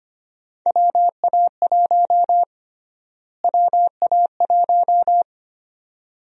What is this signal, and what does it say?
Morse code, heard as a keyed tone, about 25 wpm, 700 Hz. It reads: WA1 WA1
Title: USA Call Sign Prefixes - Repeated 2x - 25wpm